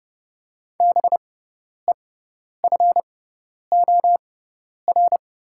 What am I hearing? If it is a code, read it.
Morse code, heard as a keyed tone, about 30 wpm, spaced as for 10 wpm, 700 Hz. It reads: BEFOR
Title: 6x Character Spacing - Top 300 Words - 30wpm